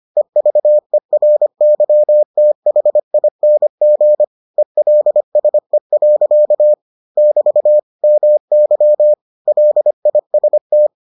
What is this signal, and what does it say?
Morse code, heard as a keyed tone, about 25 wpm, 600 Hz. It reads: EVERYTHING ELSE. <BT> MY LIST